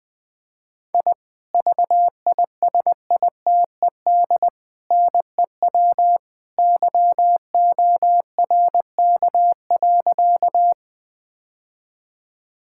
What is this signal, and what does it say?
Morse code, heard as a keyed tone, about 20 wpm, 700 Hz. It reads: I VISITED NEW YORK.
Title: Sentences with 2-Word Phrases - 20wpm